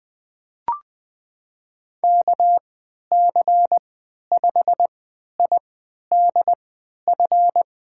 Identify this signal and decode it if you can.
Morse code, heard as a keyed tone, about 20 wpm, spaced as for 11 wpm, 700 Hz. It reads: KC5IDF
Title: USA Call Signs - 3x Character Spacing - 20wpm